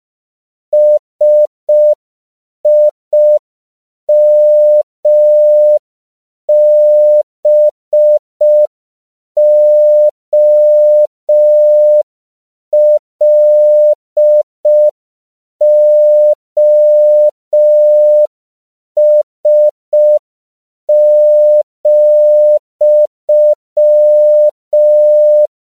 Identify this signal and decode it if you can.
Morse code, heard as a keyed tone, about 5 wpm, 600 Hz. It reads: SIMBOLOS,
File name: CW / 5PPM